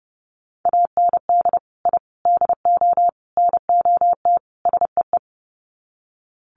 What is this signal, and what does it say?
Morse code, heard as a keyed tone, about 30 wpm, 700 Hz. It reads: AND I DO NOT SEE
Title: Sets of 5 Words Encore2 30wpm